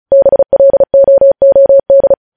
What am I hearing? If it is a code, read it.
Morse code, heard as a keyed tone, about 36 wpm, 550 Hz. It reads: BLOOD
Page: news headlines in morse code at 35 wpm